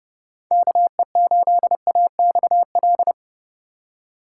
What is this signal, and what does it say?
Morse code, heard as a keyed tone, about 30 wpm, 700 Hz. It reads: KE8AXL
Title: USA Call Signs - 6 Characters - 2x Character Spacing - 30wpm